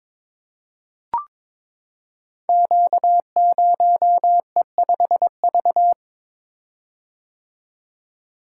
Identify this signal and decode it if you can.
Morse code, heard as a keyed tone, about 22 wpm, 700 Hz. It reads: Q0E5V
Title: Code Groups - 5 Letter:Numbers - 22wpm